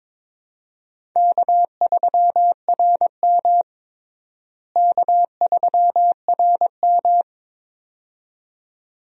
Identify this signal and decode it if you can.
Morse code, heard as a keyed tone, about 22 wpm, 700 Hz. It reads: K3RM K3RM
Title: USA Call Signs - 4 Characters - Repeated 2x - 22wpm